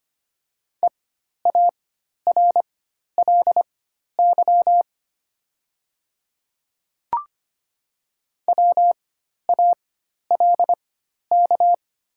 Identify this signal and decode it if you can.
Morse code, heard as a keyed tone, about 25 wpm, spaced as for 12 wpm, 700 Hz. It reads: EARLY WALK
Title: 4x Character Spacing - Top 500 Words - 25wpm